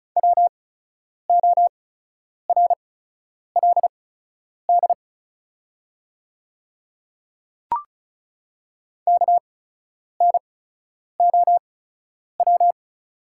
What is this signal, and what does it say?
Morse code, heard as a keyed tone, about 36 wpm, spaced as for 10 wpm, 700 Hz. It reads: WORLD KNOW
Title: 8x Character Spacing - Top 200 Words - 35wpm